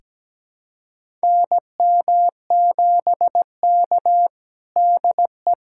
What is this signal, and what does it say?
Morse code, heard as a keyed tone, about 17 wpm, 700 Hz. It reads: NM7K DE
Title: USA Callsign DE Callsign - 4 Characters - 17wpm